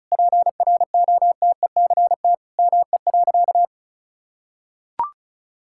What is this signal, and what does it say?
Morse code, heard as a keyed tone, about 34 wpm, 700 Hz. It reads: PROTECT ME.
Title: Sentences from Top 1600 Words 35wpm